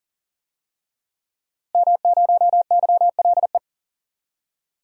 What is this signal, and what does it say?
Morse code, heard as a keyed tone, about 40 wpm, 700 Hz. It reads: M0YLE
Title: DXCC-100 Call Signs - 5 Characters - Repeated 2x - 40wpm